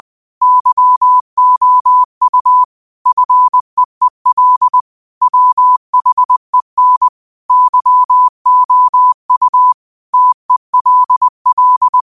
Morse code, heard as a keyed tone, about 20 wpm, 1000 Hz. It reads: YOU FEEL WHEN YOU TELL